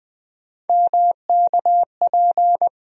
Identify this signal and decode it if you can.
Morse code, heard as a keyed tone, about 21 wpm, 700 Hz. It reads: MKP
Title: USA Call Sign Postfixes - 3 Letters 20wpm